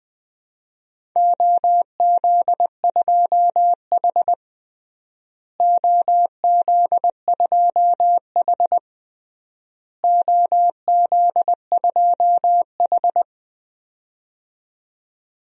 Morse code, heard as a keyed tone, about 20 wpm, 700 Hz. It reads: OZ2H OZ2H OZ2H
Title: DXCC-100 Call Signs - 4 Characters - Repeated 3x - 20wpm